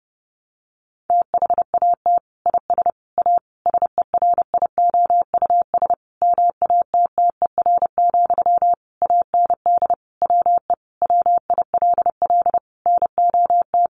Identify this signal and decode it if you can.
Morse code, heard as a keyed tone, about 30 wpm, 700 Hz. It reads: THAT IS A SERIOUS MATTER, AND WE WILL NOT
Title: Sentences from Top 1100 Words 30wpm